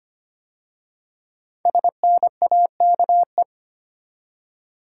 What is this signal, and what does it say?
Morse code, heard as a keyed tone, about 25 wpm, 700 Hz. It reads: SNAKE
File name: 5 Letter Words - First 3 Letters Given - 25wpm